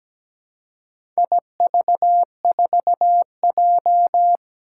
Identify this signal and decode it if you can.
Morse code, heard as a keyed tone, about 17 wpm, 700 Hz. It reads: IV4J
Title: Four Challenging Letter-Number Combinations 17wpm